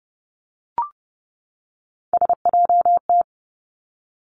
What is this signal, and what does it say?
Morse code, heard as a keyed tone, about 30 wpm, 700 Hz. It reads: SJT
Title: Three Challenging Letter-Number Combinations 30wpm